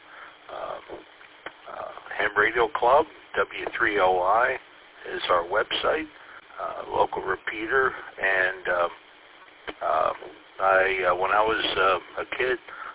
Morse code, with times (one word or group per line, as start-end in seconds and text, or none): none